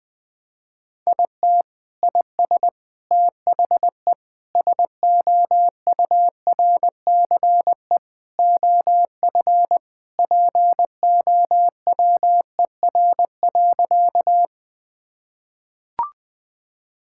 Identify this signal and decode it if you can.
Morse code, heard as a keyed tone, about 21 wpm, 700 Hz. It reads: IT IS THE SOURCE OF POWER.
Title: Sentences from Top 1000 Words - Review Part 5 - 20wpm